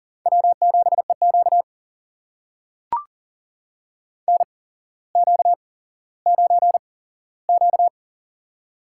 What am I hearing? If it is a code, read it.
Morse code, heard as a keyed tone, about 39 wpm, spaced as for 17 wpm, 700 Hz. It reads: W7EQ NQ9Q
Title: USA Call Signs - 4 Characters - 8x Character Spacing - 40wpm